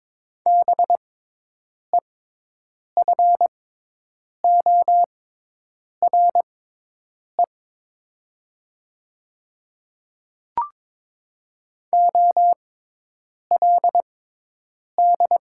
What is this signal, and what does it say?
Morse code, heard as a keyed tone, about 22 wpm, spaced as for 8 wpm, 700 Hz. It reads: BEFORE OLD
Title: 6x Character Spacing - Top 200 Words - 22wpm